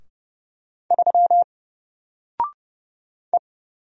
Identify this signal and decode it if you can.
Morse code, heard as a keyed tone, about 30 wpm, 700 Hz. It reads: 3 E
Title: Single Character (Letter:Number:Punctuation:Prosign) Alt 30wpm